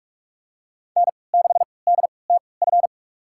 Morse code, heard as a keyed tone, about 42 wpm, spaced as for 25 wpm, 700 Hz. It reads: N6DTF
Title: USA Call Signs - 5 Characters - 3x Character Spacing - 45wpm